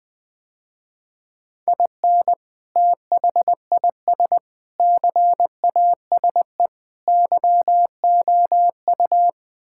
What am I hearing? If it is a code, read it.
Morse code, heard as a keyed tone, about 20 wpm, 700 Hz. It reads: IN THIS CASE YOU
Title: Sentences with 3-Word Phrases - Spoken Beginning - 20wpm